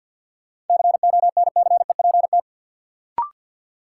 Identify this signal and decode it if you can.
Morse code, heard as a keyed tone, about 53 wpm, 700 Hz. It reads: CONCEPT
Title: Sentences from Top 1100 Words - New Word Drill - 50wpm